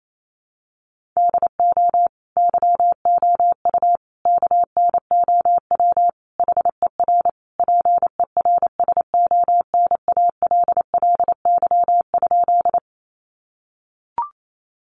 Morse code, heard as a keyed tone, about 28 wpm, 700 Hz. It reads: DO YOU KNOW HER PERSONALLY?
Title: Sentences from Top 1400 Words 28wpm